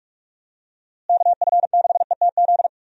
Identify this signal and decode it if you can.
Morse code, heard as a keyed tone, about 47 wpm, 700 Hz. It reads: KF6ETZ
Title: USA Call Signs - 6 Characters - Repeated 3x - 45wpm